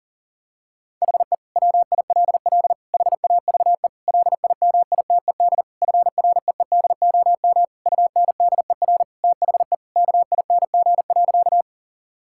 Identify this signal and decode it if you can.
Morse code, heard as a keyed tone, about 40 wpm, 700 Hz. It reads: HE WILL HAVE LIMITED FREEDOM UNDER THE KING.
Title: Sentences from Top 1000 Words - Review Part 6 - 40wpm